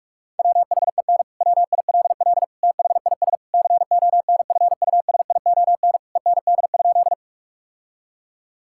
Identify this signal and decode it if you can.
Morse code, heard as a keyed tone, about 42 wpm, 700 Hz. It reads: WHEN WILL THIS CONFUSION END?
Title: Sentences from Top 1600 Words 45wpm